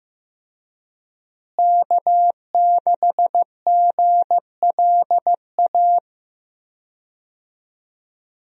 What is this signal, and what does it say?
Morse code, heard as a keyed tone, about 16 wpm, 700 Hz. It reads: K6GLA
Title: USA Call Signs - 5 Characters - Repeated 3x - 15wpm